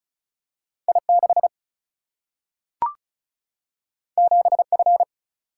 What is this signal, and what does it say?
Morse code, heard as a keyed tone, about 36 wpm, 700 Hz. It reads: I 6 7 F
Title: Single-Letter Single-Number 35wpm